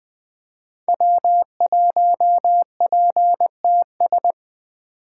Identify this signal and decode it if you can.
Morse code, heard as a keyed tone, about 20 wpm, 700 Hz. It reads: W1PTS
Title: 2023 SOTA Activator Call Signs - 20wpm